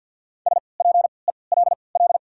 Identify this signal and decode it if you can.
Morse code, heard as a keyed tone, about 52 wpm, spaced as for 29 wpm, 700 Hz. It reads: SPELL
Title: 3x Character Spacing - Top 500 Words - 50wpm